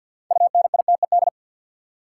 Morse code, heard as a keyed tone, about 52 wpm, 700 Hz. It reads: UNITED
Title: Sentences from Top 600 Words - New Word Drill - 50wpm